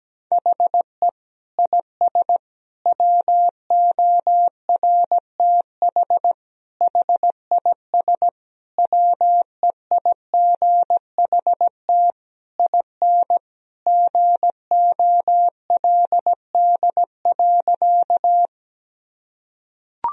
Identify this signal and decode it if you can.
Morse code, heard as a keyed tone, about 17 wpm, 700 Hz. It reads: HE IS WORTH HIS WEIGHT IN GOLD.
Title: Sentences from Top 1000 Words - Review Part 3 - 17wpm